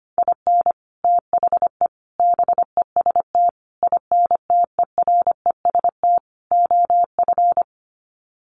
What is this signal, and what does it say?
Morse code, heard as a keyed tone, about 25 wpm, 700 Hz. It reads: IN THE BEST INTEREST OF
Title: Sets of 5 Words Encore 25wpm